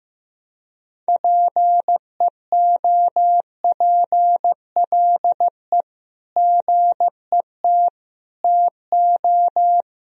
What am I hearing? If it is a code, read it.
Morse code, heard as a keyed tone, about 15 wpm, 700 Hz. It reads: PEOPLE GET TO
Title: Sentences from Top 1000 Words - Review Part 1 - 15wpm